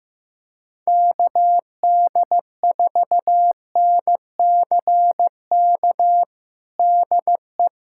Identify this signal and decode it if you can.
Morse code, heard as a keyed tone, about 15 wpm, 700 Hz. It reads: KD4NCK DE
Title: USA Callsign DE Callsign 15wpm